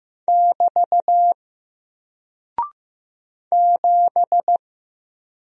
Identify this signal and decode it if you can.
Morse code, heard as a keyed tone, about 15 wpm, 700 Hz. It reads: <BT> 7
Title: Single Character (Letter:Number:Punctuation:Prosign) 15wpm